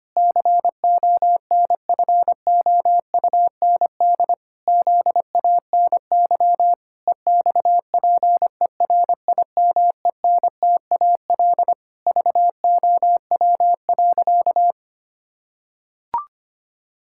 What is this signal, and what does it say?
Morse code, heard as a keyed tone, about 25 wpm, 700 Hz. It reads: CONFOUND ZANY EXPERIMENTAL VOW.
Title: Pangram Challenge v2 - 25wpm